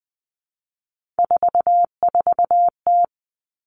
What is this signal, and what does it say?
Morse code, heard as a keyed tone, about 20 wpm, 700 Hz. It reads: 44T
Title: Three Challenging Letter-Number Combinations 20wpm